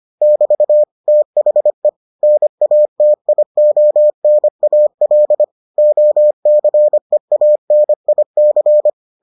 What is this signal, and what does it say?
Morse code, heard as a keyed tone, about 25 wpm, 600 Hz. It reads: <BT> THE NATIONAL OCEANIC